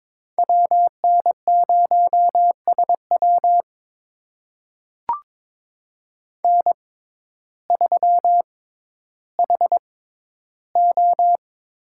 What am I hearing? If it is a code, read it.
Morse code, heard as a keyed tone, about 22 wpm, spaced as for 12 wpm, 700 Hz. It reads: WN0SW N3HO
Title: USA Call Signs - 5 Characters - 6x Character Spacing - 22wpm